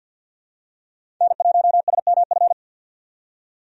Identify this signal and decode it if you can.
Morse code, heard as a keyed tone, about 47 wpm, 700 Hz. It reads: N1SMF
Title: USA Call Signs - 5 Characters - 50wpm